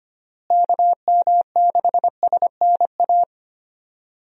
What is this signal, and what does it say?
Morse code, heard as a keyed tone, about 25 wpm, 700 Hz. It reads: KM6SNA
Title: USA Call Signs - 6 Characters - Repeated 3x - 25wpm